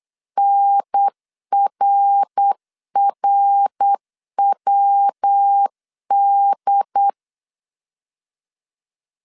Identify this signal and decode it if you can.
Morse code, heard as a keyed tone, about 8 wpm, 800 Hz. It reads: NRRWD